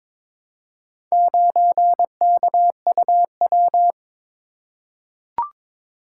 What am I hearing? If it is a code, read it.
Morse code, heard as a keyed tone, about 22 wpm, 700 Hz. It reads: 9KUW
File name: practice / 22wpm / Four Letter-Number Combinations - 22wpm